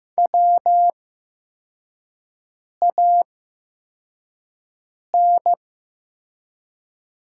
Morse code, heard as a keyed tone, about 15 wpm, spaced as for 4 wpm, 700 Hz. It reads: WAN
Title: 8x Character Spacing - Top 300 Words - 15wpm